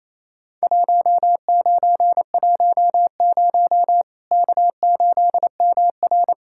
Morse code, heard as a keyed tone, about 28 wpm, 700 Hz. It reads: 1910 K8MR
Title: Worked All Europe DX Contest - QTC Exchange - 28wpm